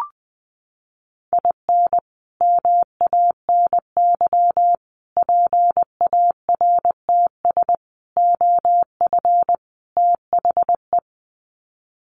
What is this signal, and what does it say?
Morse code, heard as a keyed tone, about 20 wpm, 700 Hz. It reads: IN MANY PARTS OF THE
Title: Sets of 5 Words Encore 20wpm